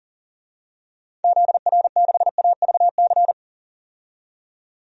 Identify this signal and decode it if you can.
Morse code, heard as a keyed tone, about 40 wpm, 700 Hz. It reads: ZR6AVC
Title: DXCC-100 Call Signs - 6 Characters - 40wpm